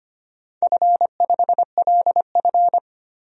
Morse code, heard as a keyed tone, about 25 wpm, 700 Hz. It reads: F5LF
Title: DXCC-100 Call Signs - 4 Characters - 25wpm